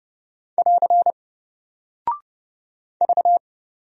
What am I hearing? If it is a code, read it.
Morse code, heard as a keyed tone, about 30 wpm, 700 Hz. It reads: <AR> V
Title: Single Character (Letter:Number:Punctuation:Prosign) Alt 30wpm